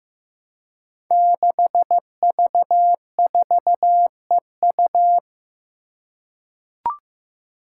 Morse code, heard as a keyed tone, about 15 wpm, 700 Hz. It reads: 6V4EU
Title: Five Challenging Letter-Number Combinations 15wpm